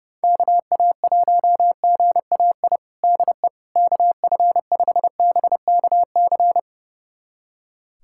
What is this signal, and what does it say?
Morse code, heard as a keyed tone, about 32 wpm, 700 Hz. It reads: KA1GAI DE KF5BKC
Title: USA Callsign DE Callsign - 6 Characters - 30wpm